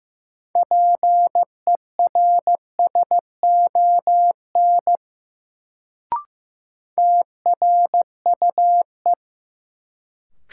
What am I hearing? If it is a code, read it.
Morse code, heard as a keyed tone, about 15 wpm, 700 Hz. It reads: PERSON TRUE
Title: Sentences from Top 400 Words - New Word Drill - 15wpm